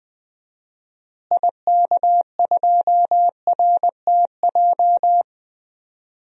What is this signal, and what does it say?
Morse code, heard as a keyed tone, about 20 wpm, 700 Hz. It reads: IK2RTJ
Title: DXCC-100 Call Signs - 6 Characters - Repeated 2x - 20wpm